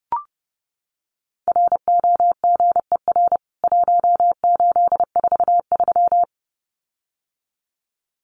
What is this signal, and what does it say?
Morse code, heard as a keyed tone, about 30 wpm, 700 Hz. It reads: ROGER 1843
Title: CWT Exchange 30wpm